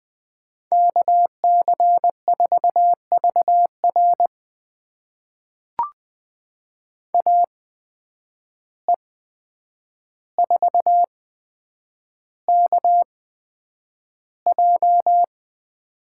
Morse code, heard as a keyed tone, about 20 wpm, spaced as for 9 wpm, 700 Hz. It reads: KC4VR AE4KJ
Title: USA Call Signs - 5 Characters - 8x Character Spacing - 20wpm